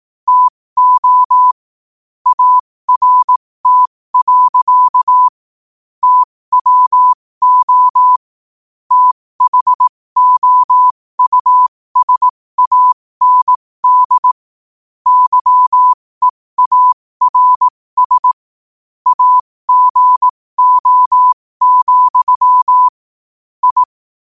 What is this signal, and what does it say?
Morse code, heard as a keyed tone, about 18 wpm, spaced as for 15 wpm, 1000 Hz. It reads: TO ART. TWO THOUSAND YEARS AGO, I
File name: qotdfiles / 15wpm